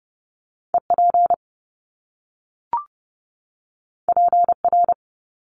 Morse code, heard as a keyed tone, about 30 wpm, 700 Hz. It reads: EP PR
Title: Top Two Letter Combinations in Words 30wpm